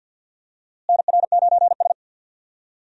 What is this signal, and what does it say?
Morse code, heard as a keyed tone, about 52 wpm, 700 Hz. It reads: NR9S
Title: USA Call Signs - 4 Characters - Repeated 2x - 50wpm